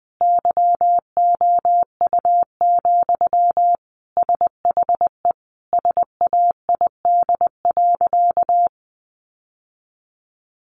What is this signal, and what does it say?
Morse code, heard as a keyed tone, about 20 wpm, 700 Hz. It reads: YOU, SHE SAID.
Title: Sentences from Top 600 Words 20wpm